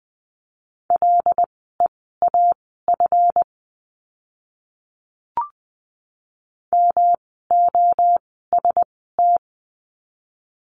Morse code, heard as a keyed tone, about 20 wpm, spaced as for 14 wpm, 700 Hz. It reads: LEAF MOST
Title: 139 Lesson 23f - Letter f - Words 20wpm